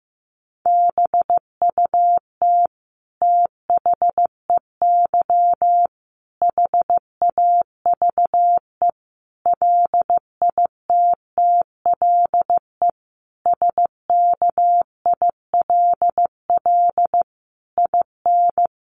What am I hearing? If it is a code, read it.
Morse code, heard as a keyed tone, about 15 wpm, 700 Hz. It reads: BUT THEY HAVE LITTLE SKILL IN
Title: Sentences from Top 1200 Words 15wpm